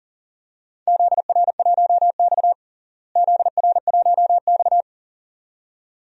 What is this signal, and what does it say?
Morse code, heard as a keyed tone, about 40 wpm, 700 Hz. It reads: ZR1X ZR1X
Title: DXCC-100 Call Signs - 4 Characters - Repeated 2x - 40wpm